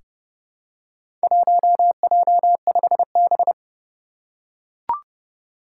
Morse code, heard as a keyed tone, about 30 wpm, 700 Hz. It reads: 1J5B
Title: Four Challenging Letter-Number Combinations 30wpm